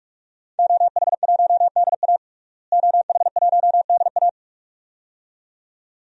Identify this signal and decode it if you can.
Morse code, heard as a keyed tone, about 46 wpm, 700 Hz. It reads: OH1DA OH1DA
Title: DXCC-100 Call Signs - 5 Characters - Repeated 2x - 45wpm